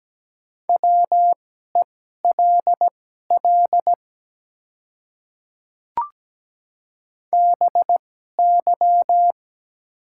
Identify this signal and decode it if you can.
Morse code, heard as a keyed tone, about 17 wpm, spaced as for 12 wpm, 700 Hz. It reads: WELL BY
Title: 2x Character Spacing - Top 200 Words - 17wpm